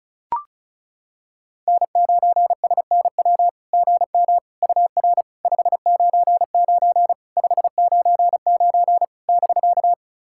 Morse code, heard as a keyed tone, about 34 wpm, 700 Hz. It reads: N9SNW GM UR 599 599 <BK>
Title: SOTA QSO Practice 35wpm